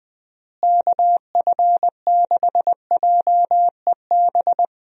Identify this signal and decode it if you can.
Morse code, heard as a keyed tone, about 20 wpm, 700 Hz. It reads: KF6JEB
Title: USA Call Signs - 6 Characters - Repeated 2x - 20wpm